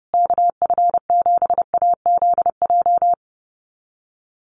Morse code, heard as a keyed tone, about 30 wpm, 700 Hz. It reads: KF7AZJ
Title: USA Call Signs Repeated 3x 30wpm